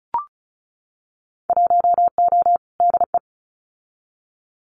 Morse code, heard as a keyed tone, about 34 wpm, 700 Hz. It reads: 1O DE
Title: Winter Field Day Exchanges 35wpm